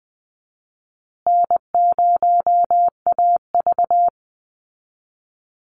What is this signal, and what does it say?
Morse code, heard as a keyed tone, about 20 wpm, 700 Hz. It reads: N0AV
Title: Easy Call Signs Repeated 3x - 20wpm